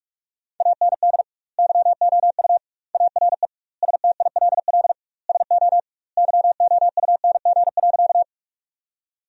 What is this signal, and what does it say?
Morse code, heard as a keyed tone, about 46 wpm, 700 Hz. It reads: AND YOU ARE STILL SO YOUNG.
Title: 2x Word-Spacing - Sentences from Top 400 Words - 45wpm